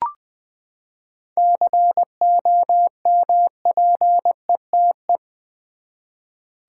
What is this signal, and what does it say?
Morse code, heard as a keyed tone, about 20 wpm, 700 Hz. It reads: COMPETE
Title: Seven Letter Words - 20wpm